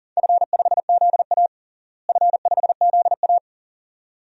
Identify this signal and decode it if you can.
Morse code, heard as a keyed tone, about 40 wpm, 700 Hz. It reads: F5ZA F5ZA
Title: DXCC-100 Call Signs - 4 Characters - Repeated 2x - 40wpm